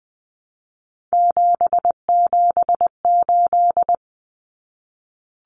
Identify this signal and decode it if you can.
Morse code, heard as a keyed tone, about 20 wpm, 700 Hz. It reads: 778
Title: Numbers - 3-Digit 20wpm